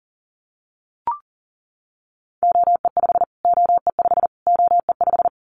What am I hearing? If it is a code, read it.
Morse code, heard as a keyed tone, about 40 wpm, 700 Hz. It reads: OE5 OE5 OE5
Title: Callsign Prefixes for Top 25 DXCC - Repeated 3x - 40wpm